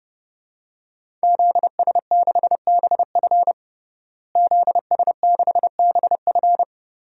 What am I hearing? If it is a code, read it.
Morse code, heard as a keyed tone, about 30 wpm, 700 Hz. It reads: ZS6BF ZS6BF
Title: DXCC-100 Call Signs - 5 Characters - Repeated 2x - 30wpm